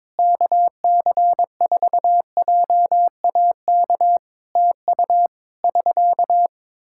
Morse code, heard as a keyed tone, about 21 wpm, 700 Hz. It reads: KC4JAK TU <SK>
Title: K1USN SST QSO Practice 22wpm